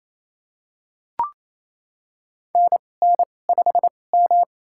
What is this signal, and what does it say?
Morse code, heard as a keyed tone, about 28 wpm, spaced as for 20 wpm, 700 Hz. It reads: NN5M
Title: USA Call Signs - 4 Characters - 2x Character Spacing - 28wpm